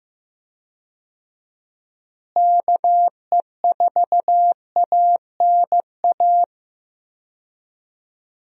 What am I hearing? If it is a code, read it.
Morse code, heard as a keyed tone, about 15 wpm, 700 Hz. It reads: KE4ANA
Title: USA Call Signs - 6 Characters - Repeated 3x - 15wpm